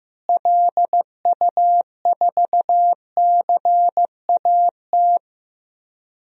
Morse code, heard as a keyed tone, about 15 wpm, 700 Hz. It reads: LU4CAT
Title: DXCC-100 Call Signs - 6 Characters - Repeated 2x - 15wpm